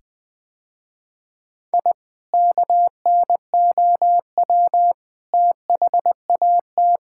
Morse code, heard as a keyed tone, about 20 wpm, 700 Hz. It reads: I KNOW THAT
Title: Sentences with 2-Word Phrases - Spoken Beginning - 20wpm